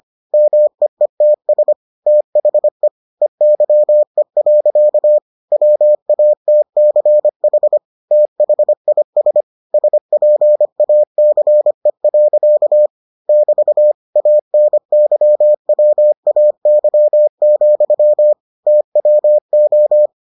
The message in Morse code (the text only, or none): MEETS THE EYE. WATCH THIS SPACE. <BT> ANYWAY, TWO